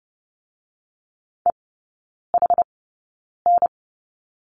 Morse code, heard as a keyed tone, about 30 wpm, spaced as for 9 wpm, 700 Hz. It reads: EHN
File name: 081 Lesson 14e - Number 2 - Three Characters 30wpm